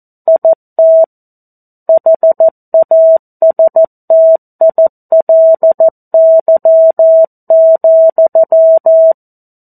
Morse code, heard as a keyed tone, about 14 wpm, 650 Hz. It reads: IT HASTILY,
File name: 02-Chapter-02 14 WPM